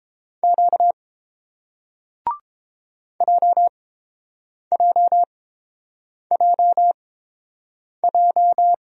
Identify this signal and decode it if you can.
Morse code, heard as a keyed tone, about 30 wpm, spaced as for 7 wpm, 700 Hz. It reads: Q JJJJ